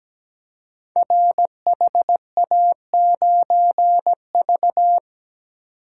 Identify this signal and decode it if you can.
Morse code, heard as a keyed tone, about 17 wpm, 700 Hz. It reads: RHA9V
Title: Code Groups - 5 Letter:Numbers - 17wpm